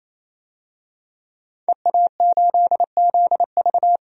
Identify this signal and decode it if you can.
Morse code, heard as a keyed tone, about 28 wpm, 700 Hz. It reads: EA8ZV
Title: DXCC-100 Call Signs - 5 Characters - 28wpm